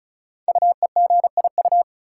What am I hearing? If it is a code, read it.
Morse code, heard as a keyed tone, about 36 wpm, 700 Hz. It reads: UEGIU
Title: Code Groups - 5 Letters - 35wpm